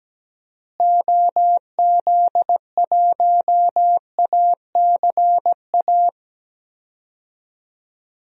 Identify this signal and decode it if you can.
Morse code, heard as a keyed tone, about 17 wpm, 700 Hz. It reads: OZ1ACA
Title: DXCC-100 Call Signs - 6 Characters - Repeated 3x - 17wpm